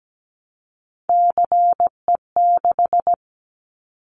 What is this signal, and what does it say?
Morse code, heard as a keyed tone, about 17 wpm, 700 Hz. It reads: CE6
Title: Callsign Prefixes for Top 100 DXCC - 17wpm